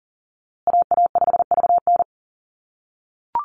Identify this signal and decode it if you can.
Morse code, heard as a keyed tone, about 40 wpm, 700 Hz. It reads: AA5VN
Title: USA Call Signs Repeated 3x 40wpm